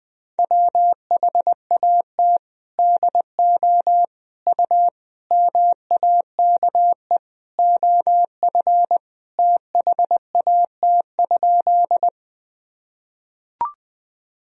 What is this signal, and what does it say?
Morse code, heard as a keyed tone, about 20 wpm, 700 Hz. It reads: WHAT DO U MAKE OF THAT?
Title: Sentences from Top 100 Words and Abbreviations 20wpm